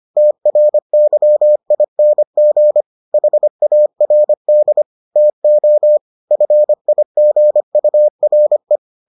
Morse code, heard as a keyed tone, about 25 wpm, 600 Hz. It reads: TRYING HARD TO FIGURE